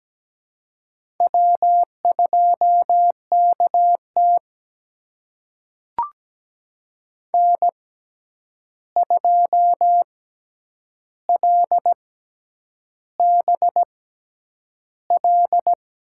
Morse code, heard as a keyed tone, about 17 wpm, spaced as for 8 wpm, 700 Hz. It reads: W2KT N2LBL
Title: USA Call Signs - 6x Character Spacing - 17wpm